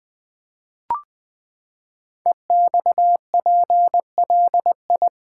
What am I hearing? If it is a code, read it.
Morse code, heard as a keyed tone, about 20 wpm, 700 Hz. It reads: EXPLI
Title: Build A Word - 5 Letters At A Time - 10-Letter Words - 20wpm